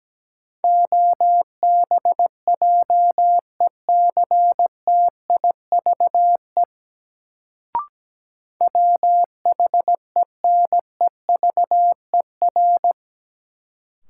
Morse code, heard as a keyed tone, about 17 wpm, 700 Hz. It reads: OBJECTIVE WHENEVER
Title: Sentences from Top 1200 Words - New Word Drill - 17wpm